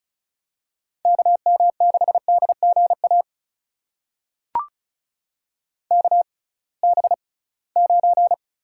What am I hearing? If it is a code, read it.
Morse code, heard as a keyed tone, about 36 wpm, spaced as for 23 wpm, 700 Hz. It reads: KM6DGA KB9
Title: USA Call Signs - 6 Characters - 6x Character Spacing - 35wpm